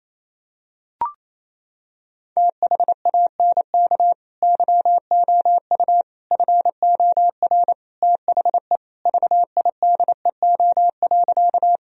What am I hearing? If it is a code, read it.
Morse code, heard as a keyed tone, about 28 wpm, 700 Hz. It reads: THANK YOU FOR THE VIDEO.
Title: Sentences from Top 1000 Words - Review Part 4 - 28wpm